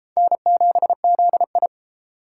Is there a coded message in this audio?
Yes: Morse code from 0.2 to 1.7 s, heard as a keyed tone, about 31 wpm, 700 Hz. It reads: N7ZI